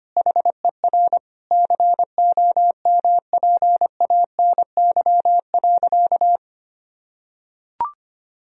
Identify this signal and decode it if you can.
Morse code, heard as a keyed tone, about 25 wpm, 700 Hz. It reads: HER COMPANY.
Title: Sentences from Top 1000 Words - Review Part 1 - 25wpm